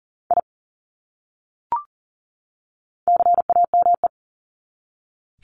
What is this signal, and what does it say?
Morse code, heard as a keyed tone, about 40 wpm, 700 Hz. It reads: I CAME
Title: Top 300 Words 40wpm